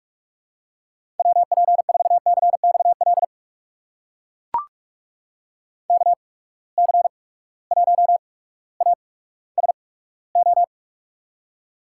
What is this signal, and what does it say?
Morse code, heard as a keyed tone, about 43 wpm, spaced as for 19 wpm, 700 Hz. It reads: WP4CXL KC1ASO
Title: USA Call Signs - 6 Characters - 8x Character Spacing - 45wpm